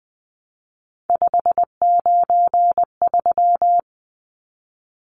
Morse code, heard as a keyed tone, about 20 wpm, 700 Hz. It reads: 593
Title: Numbers - 3-Digit 20wpm